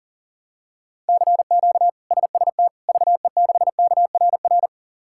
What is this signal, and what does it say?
Morse code, heard as a keyed tone, about 40 wpm, 700 Hz. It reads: CQ SST VE6KRR
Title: K1USN SST QSO Practice - 40wpm